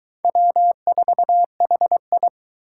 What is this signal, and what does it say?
Morse code, heard as a keyed tone, about 23 wpm, 700 Hz. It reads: W4HI